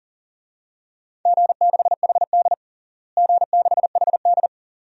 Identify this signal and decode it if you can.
Morse code, heard as a keyed tone, about 40 wpm, 700 Hz. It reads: G6HD G6HD
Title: DXCC-100 Call Signs - 4 Characters - Repeated 2x - 40wpm